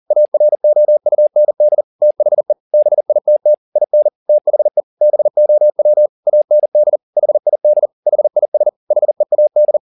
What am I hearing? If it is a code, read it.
Morse code, heard as a keyed tone, about 40 wpm, 600 Hz. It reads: AROUND THE BITT IN THE BOW AND HID HIS HEAD
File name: the old man and the sea 2 40wpm